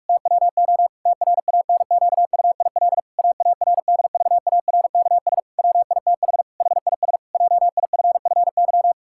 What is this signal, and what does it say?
Morse code, heard as a keyed tone, about 41 wpm, 700 Hz. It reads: TWO TRANQUIL AARDVARKS WITH HIS JIFFY
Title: Pangram Challenge v2 - 45wpm